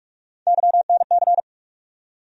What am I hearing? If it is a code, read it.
Morse code, heard as a keyed tone, about 41 wpm, 700 Hz. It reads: YNC
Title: USA Call Sign Postfixes - 3 Letters 45wpm